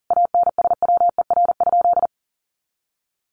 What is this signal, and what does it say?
Morse code, heard as a keyed tone, about 40 wpm, 700 Hz. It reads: ANSWER?
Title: Sentences from Top 300 Words 40wpm